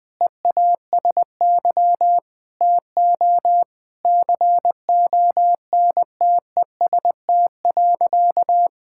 Morse code, heard as a keyed tone, about 20 wpm, 700 Hz. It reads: EASY TO CONTEST.
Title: Sentences from Top 1600 Words 20wpm